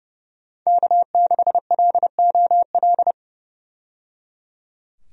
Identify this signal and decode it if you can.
Morse code, heard as a keyed tone, about 30 wpm, 700 Hz. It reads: K6LOL
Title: USA Call Signs - 5 Characters - 30wpm